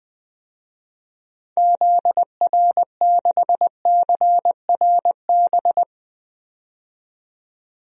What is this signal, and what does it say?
Morse code, heard as a keyed tone, about 21 wpm, 700 Hz. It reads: ZR6CRB
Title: DXCC-100 Call Signs - 6 Characters - 20wpm